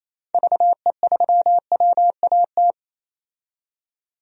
Morse code, heard as a keyed tone, about 28 wpm, 700 Hz. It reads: VE3WAT
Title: DXCC-100 Call Signs - 6 Characters - 28wpm